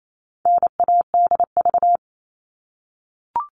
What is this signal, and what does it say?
Morse code, heard as a keyed tone, about 28 wpm, 700 Hz. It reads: NADV
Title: All Cut Numbers - 4-Digit 28wpm